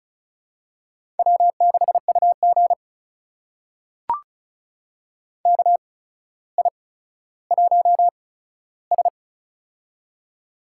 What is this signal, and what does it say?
Morse code, heard as a keyed tone, about 36 wpm, spaced as for 15 wpm, 700 Hz. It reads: W6UG KI1S
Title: USA Call Signs - 4 Characters - 8x Character Spacing - 35wpm